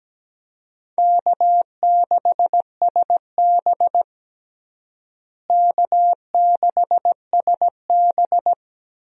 Morse code, heard as a keyed tone, about 17 wpm, 700 Hz. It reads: K6SB K6SB
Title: USA Call Signs - 4 Characters - Repeated 3x - 17wpm